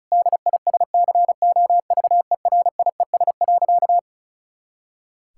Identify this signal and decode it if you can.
Morse code, heard as a keyed tone, about 36 wpm, 700 Hz. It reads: DISCOVERIES.
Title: Pangram Challenge v2 - 35wpm